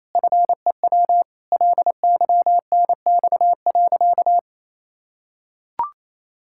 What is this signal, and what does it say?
Morse code, heard as a keyed tone, about 28 wpm, 700 Hz. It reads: FEW LYNX.
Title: Pangram Challenge v2 - 28wpm